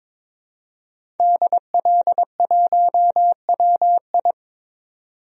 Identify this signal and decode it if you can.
Morse code, heard as a keyed tone, about 22 wpm, 700 Hz. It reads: DL1WI
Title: DXCC-100 Call Signs - 5 Characters - 22wpm